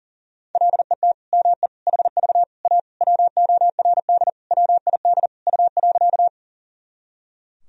Morse code, heard as a keyed tone, about 40 wpm, 700 Hz. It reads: LET ME HV A WORD WID U.
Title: Sentences from Top 300 Words and Abbreviations 40wpm